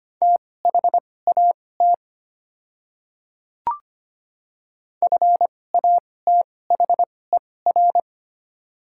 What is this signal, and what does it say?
Morse code, heard as a keyed tone, about 25 wpm, spaced as for 18 wpm, 700 Hz. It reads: THAT FATHER
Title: 2x Character Spacing - Top 300 Words - 25wpm